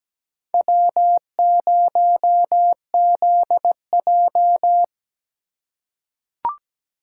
Morse code, heard as a keyed tone, about 17 wpm, 700 Hz. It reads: W0ZJ
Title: USA Call Signs - 4 Characters - 4x Character Spacing - 17wpm